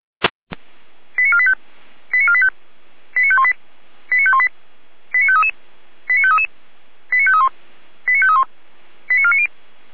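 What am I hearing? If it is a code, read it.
Five-tone selective call (ZVEI), heard as quick runs of five tones, tones of about 70 ms. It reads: 89485 89485 89518 89518 89533 89533 89531 89531 89490
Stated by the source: ZVEI five-tone call received on SDR.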